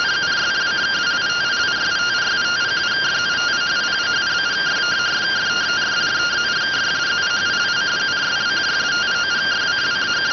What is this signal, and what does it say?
Two-tone teleprinter signal, about 170 Hz shift.